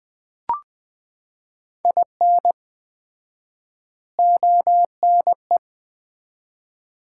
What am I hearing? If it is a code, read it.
Morse code, heard as a keyed tone, about 20 wpm, 700 Hz. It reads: IN ONE
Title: 4x Word-Spacing - Sets of 4 Words - 20wpm